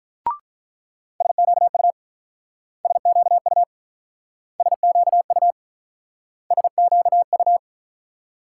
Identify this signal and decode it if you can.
Morse code, heard as a keyed tone, about 46 wpm, 700 Hz. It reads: SQU SQU SQU SQU